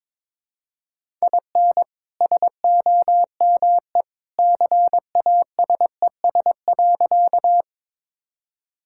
Morse code, heard as a keyed tone, about 21 wpm, 700 Hz. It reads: IN SOME CASES.
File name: practice / 22wpm / Sentences with 2-Word Phrases - 22wpm